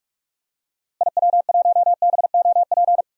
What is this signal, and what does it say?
Morse code, heard as a keyed tone, about 46 wpm, 700 Hz. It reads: IW1BOP
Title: DXCC-100 Call Signs - 6 Characters - Repeated 2x - 45wpm